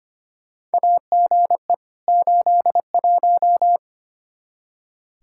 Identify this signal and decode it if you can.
Morse code, heard as a keyed tone, about 25 wpm, 700 Hz. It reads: AGE 81
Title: Practice QSO - Exchange 3 - More Info - 25wpm